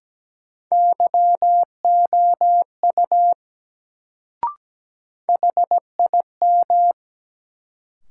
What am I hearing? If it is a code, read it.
Morse code, heard as a keyed tone, about 17 wpm, 700 Hz. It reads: YOU HIM
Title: Sentences from Top 100 Words - New Word Drill - 17wpm